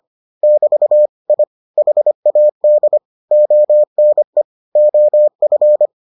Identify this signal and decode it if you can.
Morse code, heard as a keyed tone, about 25 wpm, 600 Hz. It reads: <BT> I HAD ONE OF